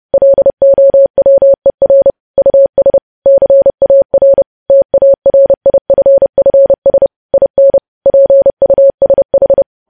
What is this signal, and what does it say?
Morse code, heard as a keyed tone, about 30 wpm, 550 Hz. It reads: LOWER US CAR TARIFFS IN PUSH
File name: podcast / News headlines-1x-30x30